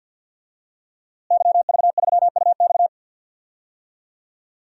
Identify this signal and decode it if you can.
Morse code, heard as a keyed tone, about 52 wpm, 700 Hz. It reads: YV3UX
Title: DXCC-100 Call Signs - 5 Characters - Repeated 3x - 50wpm